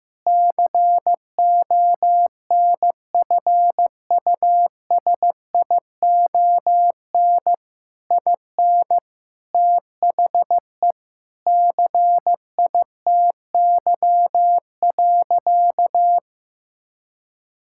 Morse code, heard as a keyed tone, about 16 wpm, 700 Hz. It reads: CONFUSION IN THE CITY.
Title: Sentences from Top 1600 Words 15wpm